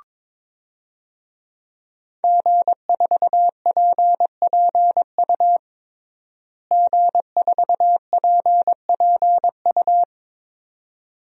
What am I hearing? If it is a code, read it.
Morse code, heard as a keyed tone, about 22 wpm, 700 Hz. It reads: G4PPU G4PPU
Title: DXCC-100 Call Signs - 5 Characters - Repeated 2x - 22wpm